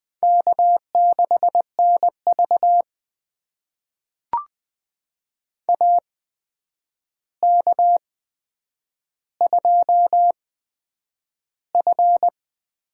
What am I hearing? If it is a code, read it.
Morse code, heard as a keyed tone, about 20 wpm, spaced as for 9 wpm, 700 Hz. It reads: K6NV AK2F
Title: USA Call Signs - 4 Characters - 8x Character Spacing - 20wpm